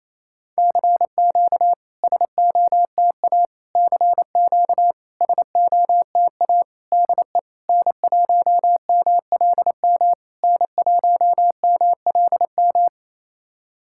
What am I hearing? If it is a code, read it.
Morse code, heard as a keyed tone, about 28 wpm, 700 Hz. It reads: CQ SOTA CQ SOTA DE N1MLM N1MLM
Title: SOTA QSO Practice 28wpm